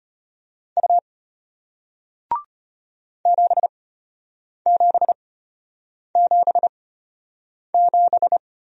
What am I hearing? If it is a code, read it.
Morse code, heard as a keyed tone, about 34 wpm, spaced as for 8 wpm, 690 Hz. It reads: U 7777